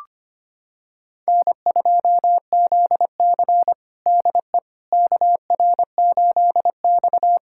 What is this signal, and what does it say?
Morse code, heard as a keyed tone, about 25 wpm, 700 Hz. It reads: N2ZC DE KR8X
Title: USA Callsign DE Callsign - 4 Characters - 25wpm